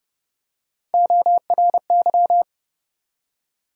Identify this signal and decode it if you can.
Morse code, heard as a keyed tone, about 32 wpm, 700 Hz. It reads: ORY
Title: USA Call Sign Postfixes - 3 Letters 30wpm